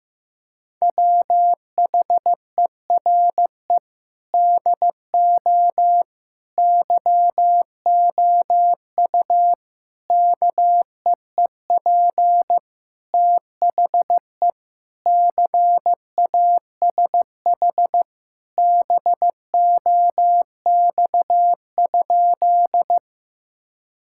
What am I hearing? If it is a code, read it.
Morse code, heard as a keyed tone, about 16 wpm, 700 Hz. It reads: WHERE DO YOU KEEP THE CASH BOX?
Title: Sentences from Top 1000 Words - Review Part 5 - 15wpm